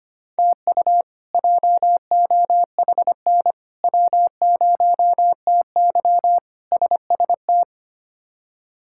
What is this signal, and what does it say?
Morse code, heard as a keyed tone, about 25 wpm, 700 Hz. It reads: TU JOHN W0TY SST
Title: K1USN SST QSO Practice - 25wpm